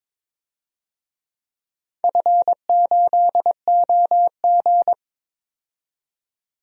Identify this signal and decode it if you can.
Morse code, heard as a keyed tone, about 22 wpm, 700 Hz. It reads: F8OG
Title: DXCC-100 Call Signs - 4 Characters - 22wpm